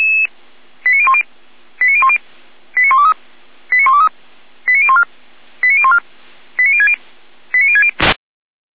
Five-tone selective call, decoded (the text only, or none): 89010 89010 89123 89123 89014 89014 89060 89060